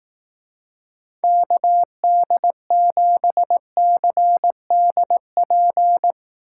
Morse code, heard as a keyed tone, about 18 wpm, 700 Hz. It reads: KD7CDP
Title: Speed Racing - USA 2x3 Call Signs 15wpm